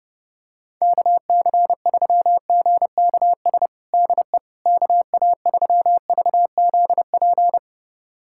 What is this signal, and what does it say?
Morse code, heard as a keyed tone, about 30 wpm, 700 Hz. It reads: KC3GKS DE KA3VZP
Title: USA Callsign DE Callsign - 6 Characters - 30wpm